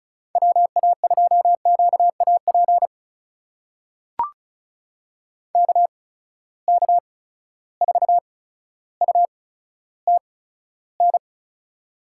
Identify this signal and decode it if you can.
Morse code, heard as a keyed tone, about 36 wpm, spaced as for 15 wpm, 700 Hz. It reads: WA2QAP KK4UTN
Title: USA Call Signs - 6 Characters - 8x Character Spacing - 35wpm